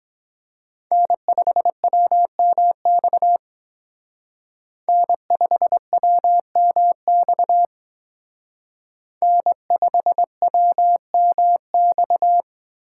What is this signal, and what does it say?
Morse code, heard as a keyed tone, about 22 wpm, 700 Hz. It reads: N5WMX N5WMX N5WMX